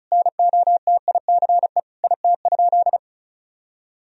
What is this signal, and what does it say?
Morse code, heard as a keyed tone, about 34 wpm, 700 Hz. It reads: NOTICE IT?
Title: Sentences from Top 1000 Words - Review Part 2 - 35wpm